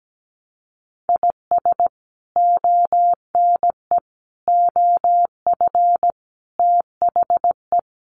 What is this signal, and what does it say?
Morse code, heard as a keyed tone, about 17 wpm, 700 Hz. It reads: IS ONE OF THE
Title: Sets of 5 Words Encore2 17wpm